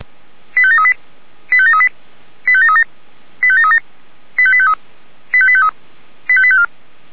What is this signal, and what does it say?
Five-tone selective call (ZVEI), heard as quick runs of five tones, tones of about 70 ms. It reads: 85738 85738 85737 85737 85853 85853 85854
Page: ZVEI